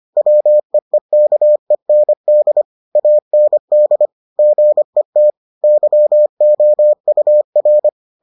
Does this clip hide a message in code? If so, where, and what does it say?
Yes: Morse code from 0.2 to 7.9 s, heard as a keyed tone, about 25 wpm, 600 Hz. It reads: WEEKEND AND GET YOUR